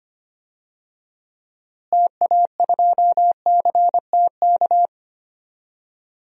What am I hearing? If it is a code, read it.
Morse code, heard as a keyed tone, about 25 wpm, 700 Hz. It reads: TA2CTK
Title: DXCC-100 Call Signs - 6 Characters - Repeated 3x - 25wpm